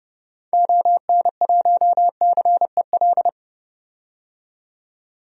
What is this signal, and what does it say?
Morse code, heard as a keyed tone, about 32 wpm, 700 Hz. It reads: ON1CEL
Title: DXCC-100 Call Signs - 6 Characters - 30wpm